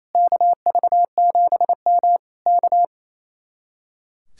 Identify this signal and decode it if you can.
Morse code, heard as a keyed tone, about 26 wpm, 700 Hz. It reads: KV7M K
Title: Speed-Racing Common QSO Phrases Encore - 28wpm